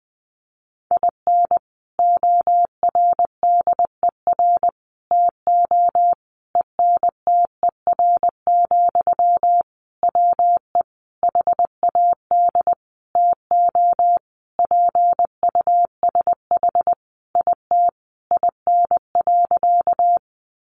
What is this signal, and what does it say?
Morse code, heard as a keyed tone, about 20 wpm, 700 Hz. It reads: IN ORDER TO ENTER, WE HAD TO PUSH IT IN.
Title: Sentences from Top 1300 Words 20wpm